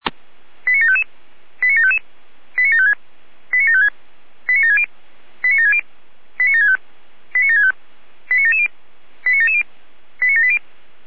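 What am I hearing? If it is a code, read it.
Five-tone selective call (ZVEI), heard as quick runs of five tones, tones of about 70 ms. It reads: 89755 89755 89756 89756 89760 89760 89765 89765 89770 89770 89780
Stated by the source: ZVEI five-tone selective call received on an SDR.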